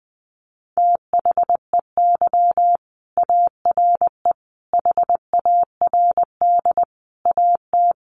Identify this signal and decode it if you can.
Morse code, heard as a keyed tone, about 20 wpm, 700 Hz. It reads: THEY ARE HARD AT
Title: Sentences from Top 300 Words 20wpm